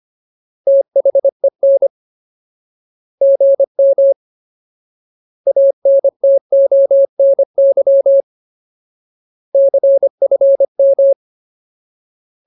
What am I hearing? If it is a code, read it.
Morse code, heard as a keyed tone, about 25 wpm, 550 Hz. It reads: THEN GM ANTONY CFM
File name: Ultimate Word List / 25WPM Ultimate